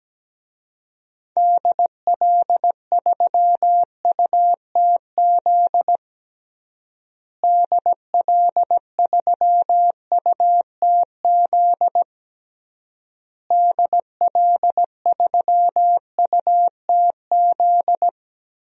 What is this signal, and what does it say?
Morse code, heard as a keyed tone, about 17 wpm, 700 Hz. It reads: DL3UTZ DL3UTZ DL3UTZ
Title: DXCC-100 Call Signs - 6 Characters - Repeated 3x - 17wpm